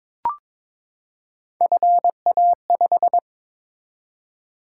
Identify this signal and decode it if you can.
Morse code, heard as a keyed tone, about 22 wpm, 700 Hz. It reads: FA5
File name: practice / 22wpm / Three Letter-Number Combinations - 22wpm